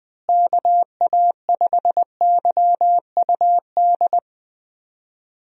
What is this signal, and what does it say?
Morse code, heard as a keyed tone, about 20 wpm, 700 Hz. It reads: KA5YUD
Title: USA Call Signs - 6 Characters - Repeated 2x - 20wpm